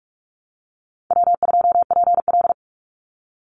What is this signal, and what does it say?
Morse code, heard as a keyed tone, about 44 wpm, 700 Hz. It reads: W2PL
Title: Easy Call Signs - 45wpm